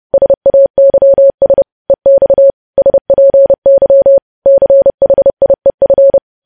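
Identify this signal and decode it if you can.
Morse code, heard as a keyed tone, about 30 wpm, 550 Hz. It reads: SAYS EX SPY CHIEF
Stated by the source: Morse code news headlines practice at 30 WPM.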